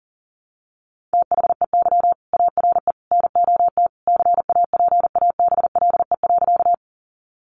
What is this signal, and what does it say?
Morse code, heard as a keyed tone, about 38 wpm, 700 Hz. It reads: THEY ARE NOT CAPABLE.
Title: Sentences from Top 1200 Words 40wpm